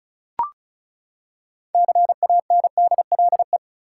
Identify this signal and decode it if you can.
Morse code, heard as a keyed tone, about 36 wpm, 700 Hz. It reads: CANDLE
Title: Six Letter Words - 35wpm